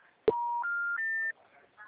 Special information tone.
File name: out- -2012- - - .95979